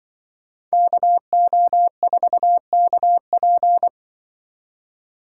Morse code, heard as a keyed tone, about 24 wpm, 700 Hz. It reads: KO4KP
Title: Speed Racing - USA 2x2 Call Signs 20wpm